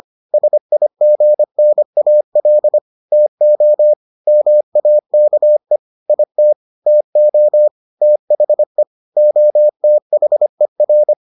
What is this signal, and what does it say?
Morse code, heard as a keyed tone, about 25 wpm, 600 Hz. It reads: SIGNAL TO MAKE IT TO THE OTHER